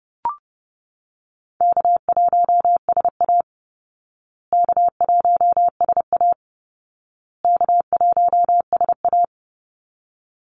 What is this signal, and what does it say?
Morse code, heard as a keyed tone, about 30 wpm, 700 Hz. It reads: K1SA K1SA K1SA
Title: Easy Call Signs Repeated 3x - 30wpm